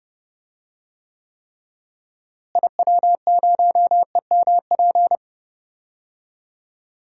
Morse code, heard as a keyed tone, about 30 wpm, 700 Hz. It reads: IW0EMP
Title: DXCC-100 Call Signs - 6 Characters - Repeated 3x - 30wpm